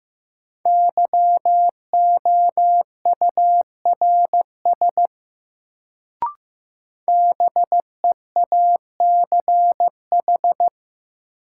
Morse code, heard as a keyed tone, about 15 wpm, 700 Hz. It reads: YOURS BEACH
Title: Sentences from Top 1500 Words - New Word Drill - 15wpm